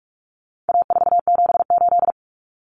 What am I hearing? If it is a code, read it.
Morse code, heard as a keyed tone, about 44 wpm, 700 Hz. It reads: A478
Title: Common Cut Numbers - 4-Digit 45wpm